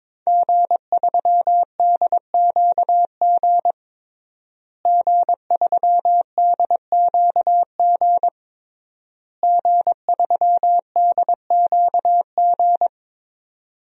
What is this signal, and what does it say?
Morse code, heard as a keyed tone, about 21 wpm, 700 Hz. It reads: G3DQG G3DQG G3DQG
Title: DXCC-100 Call Signs - 5 Characters - Repeated 3x - 22wpm